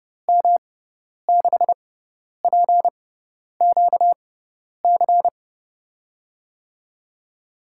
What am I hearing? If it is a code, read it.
Morse code, heard as a keyed tone, about 30 wpm, spaced as for 10 wpm, 700 Hz. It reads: M6PQC
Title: USA Call Signs - 6 Characters - 6x Character Spacing - 30wpm